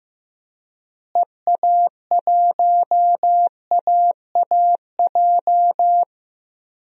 Morse code, heard as a keyed tone, about 15 wpm, 700 Hz. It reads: EA1AAJ
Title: DXCC-100 Call Signs - 6 Characters - Repeated 2x - 15wpm